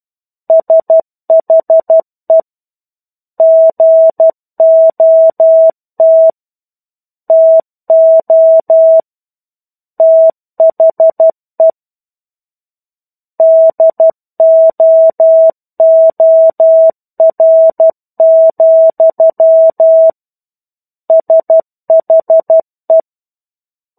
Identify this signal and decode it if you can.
Morse code, heard as a keyed tone, about 12 wpm, 650 Hz. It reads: SHE GOT TO THE DOOR, SHE
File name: 01-Chapter-01 12 WPM